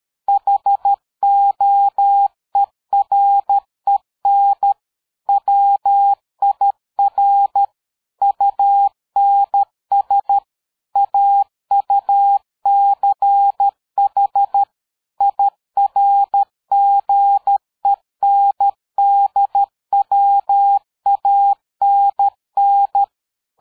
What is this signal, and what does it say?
Morse code, heard as a keyed tone, about 13 wpm, 800 Hz. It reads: HOEREN WIR UNS AUCH IRGENDWANN
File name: dk5ke / PersWorte 60